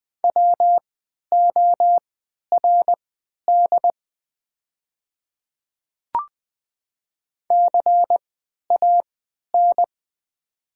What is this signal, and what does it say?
Morse code, heard as a keyed tone, about 21 wpm, spaced as for 12 wpm, 700 Hz. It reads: WORD CAN
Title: 3x Character Spacing - Top 200 Words - 20wpm